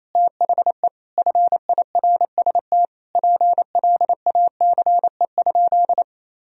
Morse code, heard as a keyed tone, about 28 wpm, 700 Hz. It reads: THE FIRST PLACE?
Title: Sentences from Top 200 Words and Abbreviations 28wpm